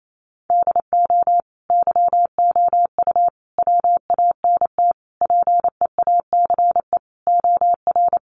Morse code, heard as a keyed tone, about 28 wpm, 700 Hz. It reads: DO YOU WANT PEACE OR
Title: Sentences from Top 1500 Words 28wpm